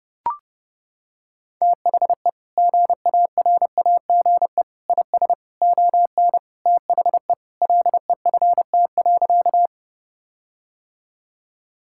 Morse code, heard as a keyed tone, about 32 wpm, 700 Hz. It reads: THE GARAGE IS ON THE LEFT.
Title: Sentences from Top 1600 Words 30wpm